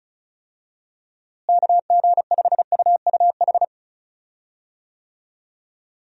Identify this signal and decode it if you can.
Morse code, heard as a keyed tone, about 34 wpm, 700 Hz. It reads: KG5UUH
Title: USA Call Signs - 6 Characters - Repeated 3x - 35wpm